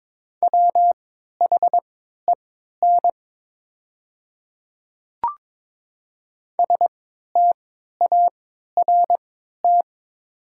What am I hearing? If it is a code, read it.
Morse code, heard as a keyed tone, about 22 wpm, spaced as for 12 wpm, 700 Hz. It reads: WHEN START
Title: 3x Character Spacing - Top 300 Words - 22wpm